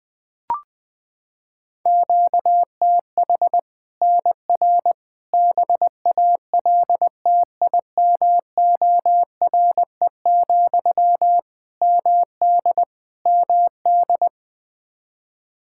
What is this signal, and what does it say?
Morse code, heard as a keyed tone, about 20 wpm, 700 Hz. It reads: QTH NR BALTIMORE, MD MD
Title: Practice QSO - Exchange 1 - Beginning - 20wpm